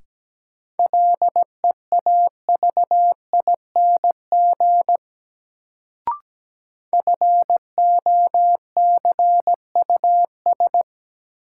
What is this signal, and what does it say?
Morse code, heard as a keyed tone, about 17 wpm, 700 Hz. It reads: LEAVING FOCUS
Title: Sentences from Top 1000 Words - Review Part 4 - New Word Drill - 17wpm